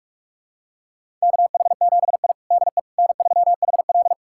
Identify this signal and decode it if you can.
Morse code, heard as a keyed tone, about 46 wpm, 700 Hz. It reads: KH7I DE N3HL
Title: USA Callsign DE Callsign - 4 Characters - 45wpm